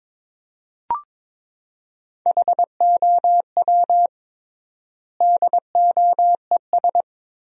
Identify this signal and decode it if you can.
Morse code, heard as a keyed tone, about 22 wpm, 700 Hz. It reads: HOW DOES
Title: 3x Word-Spacing - Sentences from Top 400 Words - 22wpm